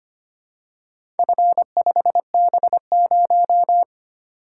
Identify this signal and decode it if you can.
Morse code, heard as a keyed tone, about 25 wpm, 700 Hz. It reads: F5B0
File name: practice / 25wpm / Four Letter-Number Combinations - 25wpm